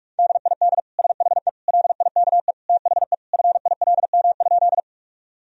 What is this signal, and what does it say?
Morse code, heard as a keyed tone, about 41 wpm, 700 Hz. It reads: DID SHE LIKE THE FILM?
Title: Sentences from Top 1000 Words - Review Part 3 - 45wpm